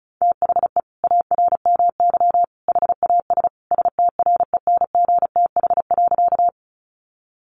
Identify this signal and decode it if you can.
Morse code, heard as a keyed tone, about 34 wpm, 700 Hz. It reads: THE ARMY HAS STRENGTH.
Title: Sentences from Top 1300 Words 35wpm